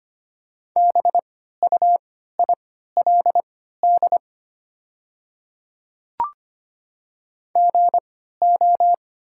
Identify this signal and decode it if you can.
Morse code, heard as a keyed tone, about 25 wpm, spaced as for 14 wpm, 700 Hz. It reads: BUILD GO
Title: 3x Character Spacing - Top 200 Words - 25wpm